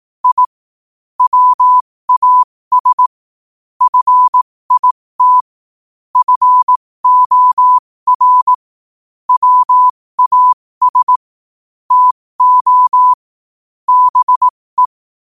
Morse code, heard as a keyed tone, about 18 wpm, spaced as for 15 wpm, 1000 Hz. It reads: I WAS FIT FOR WAS TO BE
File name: qotdfiles / 15wpm